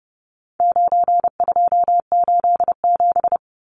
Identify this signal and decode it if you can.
Morse code, heard as a keyed tone, about 28 wpm, 700 Hz. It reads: 9287
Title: Numbers - 4-Digit 30wpm